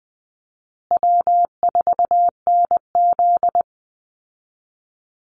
Morse code, heard as a keyed tone, about 20 wpm, 700 Hz. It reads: W4NZ
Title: Call Signs Repeated 2x - 20wpm